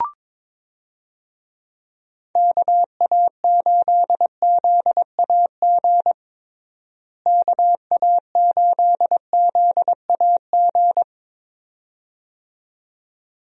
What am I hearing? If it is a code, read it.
Morse code, heard as a keyed tone, about 22 wpm, 700 Hz. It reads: KA8ZAG KA8ZAG
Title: USA Call Signs - 6 Characters - Repeated 2x - 22wpm